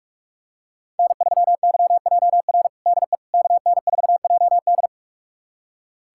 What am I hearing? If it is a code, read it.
Morse code, heard as a keyed tone, about 42 wpm, 700 Hz. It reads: N3YJR DE KN4JD
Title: USA Callsign DE Callsign - 5 Characters - 45wpm